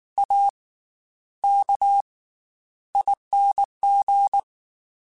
Morse code, heard as a keyed tone, about 19 wpm, 800 Hz. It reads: A K ING
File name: Special INGs / Special INGs-mp3s / Slow ing @18